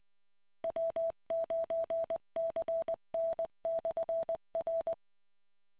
Morse code, heard as a keyed tone, about 20 wpm, 660 Hz. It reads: W9CN/R